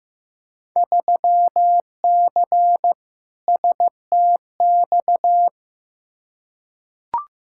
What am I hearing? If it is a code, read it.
Morse code, heard as a keyed tone, about 15 wpm, 700 Hz. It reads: 3C STX
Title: Field Day Exchanges Encore - 15wpm